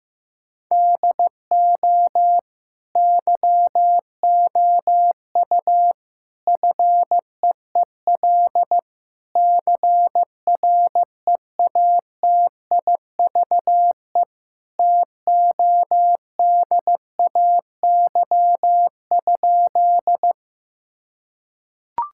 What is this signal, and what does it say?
Morse code, heard as a keyed tone, about 16 wpm, 700 Hz. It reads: DO YOU FEEL CREATIVE TODAY?
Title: Sentences from Top 1000 Words - Review Part 5 - 15wpm